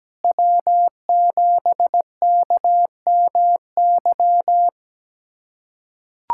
Morse code, heard as a keyed tone, about 17 wpm, 700 Hz. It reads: W7KMY
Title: USA Callsign DE Callsign - 5 Characters - 17wpm